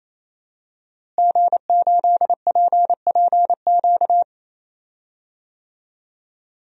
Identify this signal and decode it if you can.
Morse code, heard as a keyed tone, about 28 wpm, 700 Hz. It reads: G8PPQ
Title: DXCC-100 Call Signs - 5 Characters - Repeated 3x - 28wpm